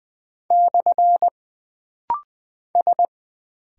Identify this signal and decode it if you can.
Morse code, heard as a keyed tone, about 20 wpm, 700 Hz. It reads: / S
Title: Single Character (Letter:Number:Punctuation:Prosign) - Rapid-Fire - 20wpm